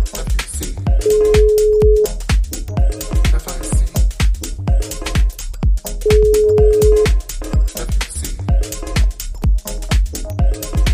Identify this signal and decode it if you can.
Ringback tone (Europe, 425 Hz).